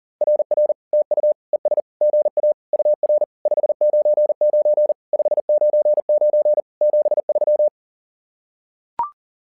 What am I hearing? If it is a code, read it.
Morse code, heard as a keyed tone, about 40 wpm, 600 Hz. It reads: RR TU ES GA UR 599 599 73
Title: SOTA QSO Practice 40wpm